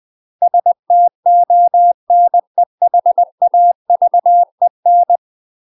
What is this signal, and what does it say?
Morse code, heard as a keyed tone, about 20 wpm, 700 Hz. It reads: STONEHAVEN
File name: UK MASTER QSOs TRAINING / 20WPM / UK MASTER QSOs 0023